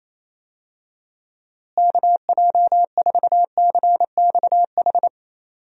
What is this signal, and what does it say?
Morse code, heard as a keyed tone, about 28 wpm, 700 Hz. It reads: KJ4CXH
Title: USA Call Signs - 6 Characters - Repeated 3x - 28wpm